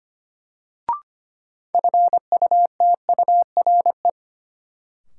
Morse code, heard as a keyed tone, about 25 wpm, 700 Hz. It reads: FUTURE
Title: Sentences from Top 700 Words - New Word Drill - 25wpm